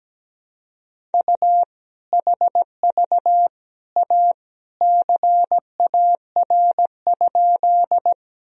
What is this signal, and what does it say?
Morse code, heard as a keyed tone, about 17 wpm, 700 Hz. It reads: U HV A CAR?
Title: Sentences from Top 300 Words and Abbreviations 17wpm